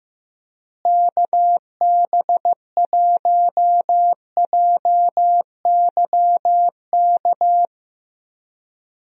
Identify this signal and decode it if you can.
Morse code, heard as a keyed tone, about 15 wpm, 700 Hz. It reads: KB1JYK
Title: USA Call Signs - 6 Characters - Repeated 2x - 15wpm